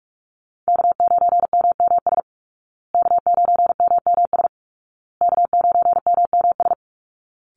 Morse code, heard as a keyed tone, about 44 wpm, 700 Hz. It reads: K9MMS K9MMS K9MMS
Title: Call Signs Repeated 2x - 45wpm